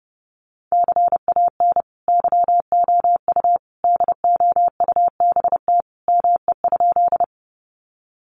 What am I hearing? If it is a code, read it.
Morse code, heard as a keyed tone, about 28 wpm, 700 Hz. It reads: CAN YOU DOUBT ME?
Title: Sentences from Top 1500 Words 30wpm